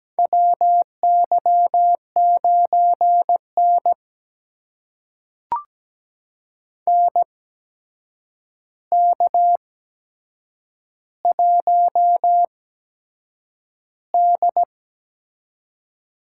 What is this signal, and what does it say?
Morse code, heard as a keyed tone, about 17 wpm, spaced as for 7 wpm, 700 Hz. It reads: WY9N NK1D
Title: USA Call Signs - 4 Characters - 8x Character Spacing - 17wpm